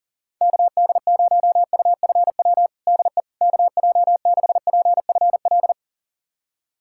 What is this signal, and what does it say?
Morse code, heard as a keyed tone, about 37 wpm, 700 Hz. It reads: KD0UFW DE KJ6PFL
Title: USA Callsign DE Callsign - 6 Characters - 40wpm